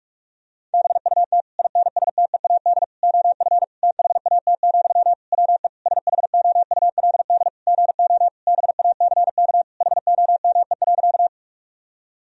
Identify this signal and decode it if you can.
Morse code, heard as a keyed tone, about 46 wpm, 700 Hz. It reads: BUT INSTEAD OF THAT, WE SHOULD GO BACK HOME.
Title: Sentences from Top 1000 Words - Review Part 4 - 45wpm